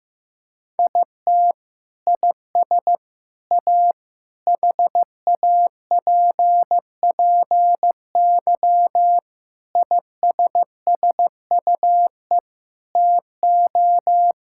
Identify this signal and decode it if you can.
Morse code, heard as a keyed tone, about 16 wpm, 700 Hz. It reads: IT IS A HAPPY ISSUE TO
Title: Sentences from Top 1000 Words - Review Part 6 - 15wpm